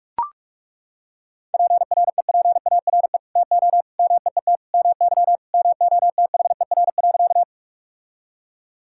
Morse code, heard as a keyed tone, about 42 wpm, 700 Hz. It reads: PREPARE TO MEET MY MOTHER.
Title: Sentences from Top 1000 Words - Review Part 6 - 45wpm